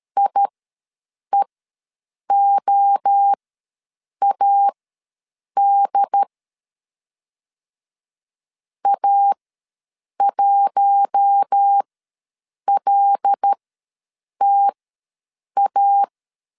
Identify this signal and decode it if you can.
Morse code, heard as a keyed tone, about 13 wpm, spaced as for 7 wpm, 800 Hz. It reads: IEOAD A1LTA
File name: LEZIONI / 5 LBVC / 5 L ws 12wpm 2958